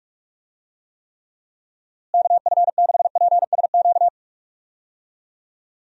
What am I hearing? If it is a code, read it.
Morse code, heard as a keyed tone, about 42 wpm, 700 Hz. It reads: KF6PSQ
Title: USA Call Signs - 6 Characters - Repeated 2x - 45wpm